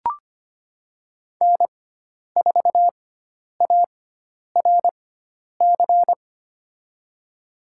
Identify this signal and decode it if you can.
Morse code, heard as a keyed tone, about 25 wpm, spaced as for 10 wpm, 700 Hz. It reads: N4ARC
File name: Lesson 16g - Letter c - Call Signs 25wpm